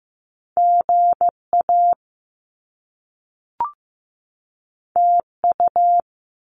Morse code, heard as a keyed tone, about 15 wpm, 700 Hz. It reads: GA TU
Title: Top Two Letter Combinations in Words 15wpm